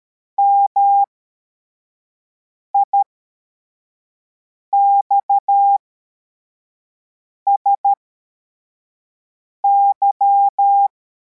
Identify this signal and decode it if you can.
Morse code, heard as a keyed tone, about 13 wpm, spaced as for 4 wpm, 800 Hz. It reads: MIXSY
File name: formation / cw / Text52